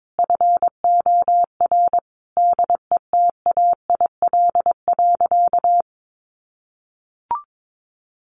Morse code, heard as a keyed tone, about 22 wpm, 700 Hz. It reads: FOR DETAIL.
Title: Sentences from Top 1500 Words 22wpm